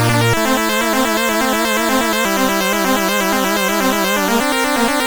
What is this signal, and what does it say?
There is no code.